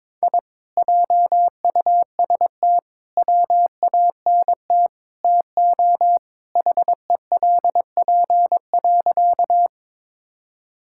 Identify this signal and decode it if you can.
Morse code, heard as a keyed tone, about 22 wpm, 700 Hz. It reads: I JUST WANT TO HELP.
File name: practice / 22wpm / Sentences with 4-Word Phrases - 22wpm